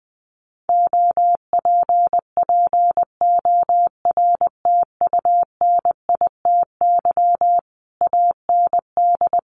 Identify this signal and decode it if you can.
Morse code, heard as a keyed tone, about 20 wpm, 700 Hz. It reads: OPPORTUNITY AND
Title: Sentences from Top 1200 Words 20wpm